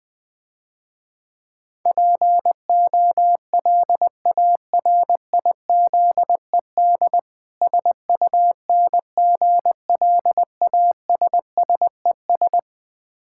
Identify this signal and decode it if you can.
Morse code, heard as a keyed tone, about 20 wpm, 700 Hz. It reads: POLARIZED SUNGLASSES
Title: Difficult Words - Speed-Racing with Hints - 20wpm